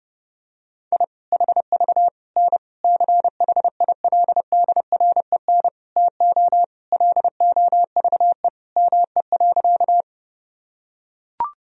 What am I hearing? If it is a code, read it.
Morse code, heard as a keyed tone, about 30 wpm, 700 Hz. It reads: I HV N CHILDREN TO LOVE ME.
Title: Sentences from Top 400 Words and Abbreviations 30wpm